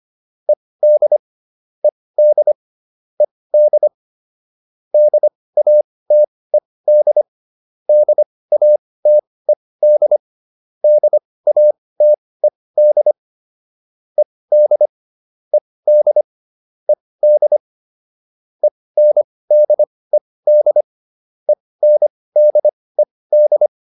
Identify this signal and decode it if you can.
Morse code, heard as a keyed tone, about 25 wpm, spaced as for 18 wpm, 600 Hz. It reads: ED ED ED DATED DATED DATED ED ED ED ENDED ENDED